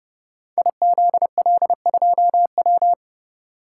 Morse code, heard as a keyed tone, about 31 wpm, 700 Hz. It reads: IZL2W
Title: Code Groups - 5 Letter:Numbers - 30wpm